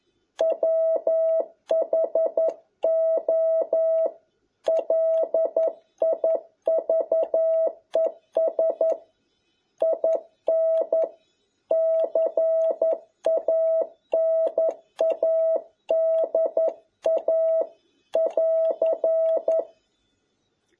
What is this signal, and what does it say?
Morse code, heard as a keyed tone, about 11 wpm, 650 Hz. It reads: WHO LIVES IN CANADA <AR>
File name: jh8cbh / 2021cw